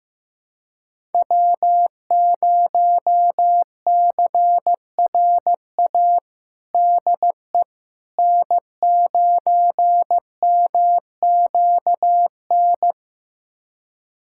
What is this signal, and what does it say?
Morse code, heard as a keyed tone, about 15 wpm, 700 Hz. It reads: W0CRA DE N9MQN
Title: USA Callsign DE Callsign - 5 Characters - 15wpm